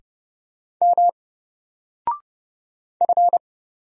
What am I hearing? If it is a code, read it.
Morse code, heard as a keyed tone, about 29 wpm, 700 Hz. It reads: M F
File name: Single-Letter-Number-30wpm